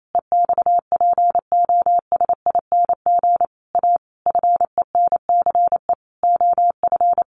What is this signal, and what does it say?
Morse code, heard as a keyed tone, about 28 wpm, 700 Hz. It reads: EXPOSING A FENCE OF
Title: A Sound Of Thunder 28wpm